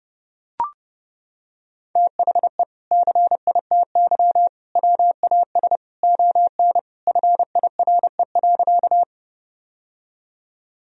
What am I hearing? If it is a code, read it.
Morse code, heard as a keyed tone, about 32 wpm, 700 Hz. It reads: THE CITY WAS ON FIRE.
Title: Sentences from Top 1000 Words - Review Part 1 - 30wpm